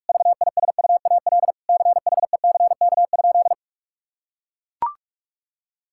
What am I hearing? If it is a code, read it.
Morse code, heard as a keyed tone, about 41 wpm, 700 Hz. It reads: VISUAL CHECK?
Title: Sentences from Top 1000 Words - Review Part 6 - 45wpm